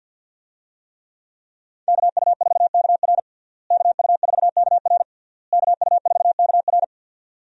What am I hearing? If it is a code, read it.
Morse code, heard as a keyed tone, about 52 wpm, 700 Hz. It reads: KU4KR KU4KR KU4KR
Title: USA Call Signs - 5 Characters - Repeated 3x - 50wpm